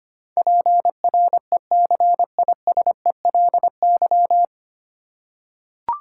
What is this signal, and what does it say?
Morse code, heard as a keyed tone, about 25 wpm, 700 Hz. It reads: PRECISELY
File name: practice / 25wpm / 9 Letter Words - First 3 Letters Given - 25wpm